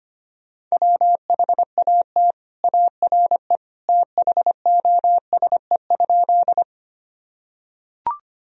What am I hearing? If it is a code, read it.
Morse code, heard as a keyed tone, about 25 wpm, 700 Hz. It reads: WHAT ARE THOSE?
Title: Sentences from Top 1000 Words - Review Part 1 - 25wpm